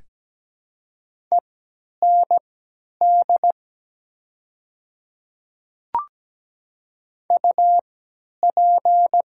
Morse code, heard as a keyed tone, about 17 wpm, spaced as for 10 wpm, 700 Hz. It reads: END UP
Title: 3x Character Spacing - Top 200 Words - 17wpm